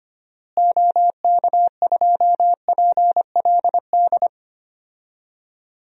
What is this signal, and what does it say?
Morse code, heard as a keyed tone, about 25 wpm, 700 Hz. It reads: OK2PLD
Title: DXCC-100 Call Signs - 6 Characters - Repeated 3x - 25wpm